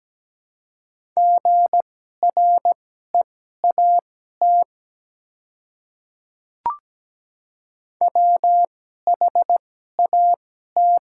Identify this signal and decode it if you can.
Morse code, heard as a keyed tone, about 17 wpm, spaced as for 12 wpm, 700 Hz. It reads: GREAT WHAT
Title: 2x Character Spacing - Top 200 Words - 17wpm